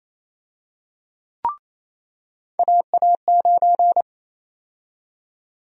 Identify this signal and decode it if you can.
Morse code, heard as a keyed tone, about 28 wpm, 700 Hz. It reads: AA9
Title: USA Call Sign Prefixes - 28wpm